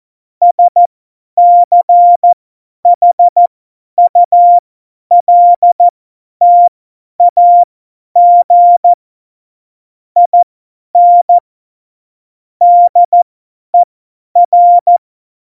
Morse code, heard as a keyed tone, about 14 wpm, spaced as for 10 wpm, 700 Hz. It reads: SCHULTAG IN DER
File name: Morsekurs / Witze / Witz 21 Juli 2020, 50 F70